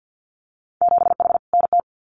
Morse code, heard as a keyed tone, about 50 wpm, 700 Hz. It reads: 7H NT
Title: Winter Field Day Exchanges Encore 50wpm